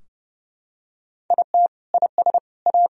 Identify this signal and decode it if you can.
Morse code, heard as a keyed tone, about 30 wpm, 700 Hz. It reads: IT IS A
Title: Sentences from Top 1000 Words - Review Part 4 - 30wpm